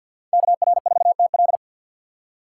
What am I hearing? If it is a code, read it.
Morse code, heard as a keyed tone, about 62 wpm, 700 Hz. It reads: KR4TL
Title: Speed Racing - USA Call Signs 50wpm